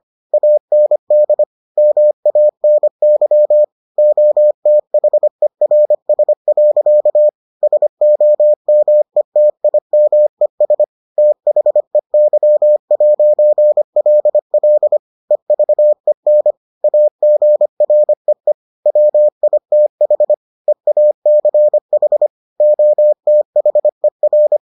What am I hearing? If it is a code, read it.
Morse code, heard as a keyed tone, about 25 wpm, 600 Hz. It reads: AND MANY OTHERS. SOMETIMES THEY'LL EVEN AGREE WITH EACH OTHER